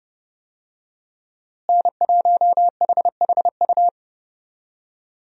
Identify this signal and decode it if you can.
Morse code, heard as a keyed tone, about 31 wpm, 700 Hz. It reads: N1HHU
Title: USA Call Signs - 5 Characters - Repeated 2x - 30wpm